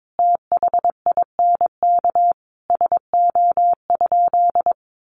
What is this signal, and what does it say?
Morse code, heard as a keyed tone, about 22 wpm, 700 Hz. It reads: THINK SO?
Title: Sentences from Top 600 Words 22wpm